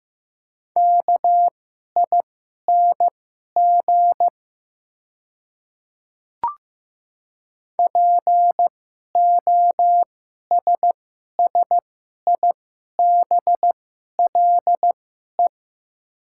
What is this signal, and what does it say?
Morse code, heard as a keyed tone, about 15 wpm, spaced as for 11 wpm, 700 Hz. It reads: KING POSSIBLE
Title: 2x Character Spacing - Top 500 Words - 15wpm